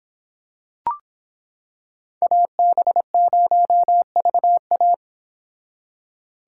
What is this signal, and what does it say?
Morse code, heard as a keyed tone, about 26 wpm, 700 Hz. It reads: AB0VA